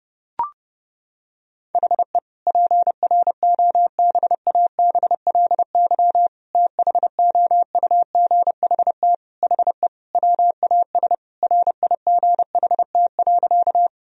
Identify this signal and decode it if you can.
Morse code, heard as a keyed tone, about 32 wpm, 700 Hz. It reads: HE PROBABLY THOUGHT HE WAS RIGHT.
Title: Sentences from Top 1000 Words - Review Part 4 - 30wpm